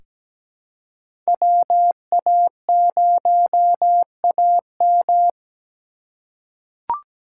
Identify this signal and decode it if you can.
Morse code, heard as a keyed tone, about 17 wpm, 700 Hz. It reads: WA0AM
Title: USA Call Signs - 5 Characters - 3x Character Spacing - 17wpm